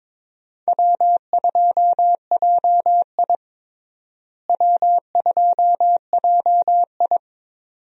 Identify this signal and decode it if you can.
Morse code, heard as a keyed tone, about 22 wpm, 700 Hz. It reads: W2JI W2JI
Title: USA Call Signs - 4 Characters - Repeated 2x - 22wpm